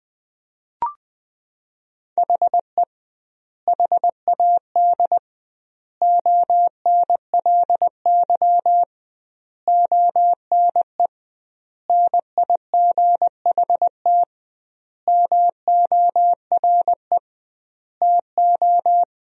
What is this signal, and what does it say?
Morse code, heard as a keyed tone, about 21 wpm, 700 Hz. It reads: HE HAD ONLY ONE NIGHT MORE TO
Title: 2x Word-Spacing - Sentences from Top 300 Words - 20wpm